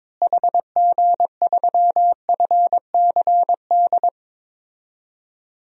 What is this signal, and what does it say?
Morse code, heard as a keyed tone, about 22 wpm, 700 Hz. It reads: HG3FCD
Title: DXCC-100 Call Signs - 6 Characters - Repeated 2x - 22wpm